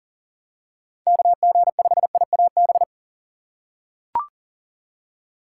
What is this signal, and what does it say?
Morse code, heard as a keyed tone, about 40 wpm, 700 Hz. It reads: KG5IAB
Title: USA Call Signs - 6 Characters - 2x Character Spacing - 40wpm